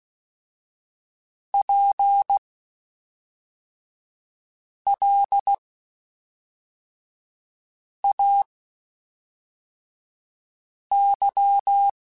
Morse code, heard as a keyed tone, about 16 wpm, spaced as for 3 wpm, 800 Hz. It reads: PLAY